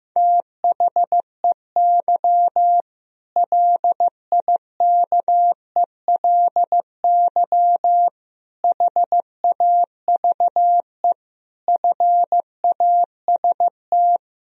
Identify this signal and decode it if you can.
Morse code, heard as a keyed tone, about 15 wpm, 700 Hz. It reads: THEY LIKELY HAVE FAST
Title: Sentences from Top 1000 Words - Review Part 4 - 15wpm